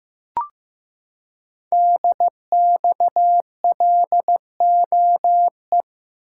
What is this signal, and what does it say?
Morse code, heard as a keyed tone, about 15 wpm, 700 Hz. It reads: DXLOE
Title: Code Groups - 5 Letters - 15wpm